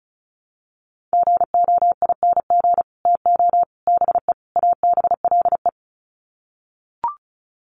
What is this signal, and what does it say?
Morse code, heard as a keyed tone, about 34 wpm, 700 Hz. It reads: GOING TO BE ABLE
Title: Sets of 4 Words 35wpm